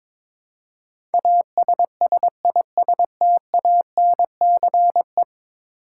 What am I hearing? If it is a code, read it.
Morse code, heard as a keyed tone, about 22 wpm, 700 Hz. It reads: ASSISTANCE
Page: morse-code-ninja-practice-22wpm